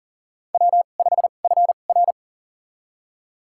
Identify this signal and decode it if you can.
Morse code, heard as a keyed tone, about 38 wpm, spaced as for 28 wpm, 690 Hz. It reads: W5FR
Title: USA Call Signs - 4 Characters - 2x Character Spacing - 40wpm